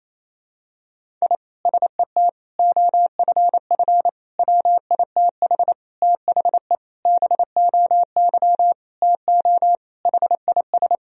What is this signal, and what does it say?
Morse code, heard as a keyed tone, about 28 wpm, 700 Hz. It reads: I SET OFF WITH THE BOY TO HIS
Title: Sentences from Top 1000 Words - Review Part 1 - 28wpm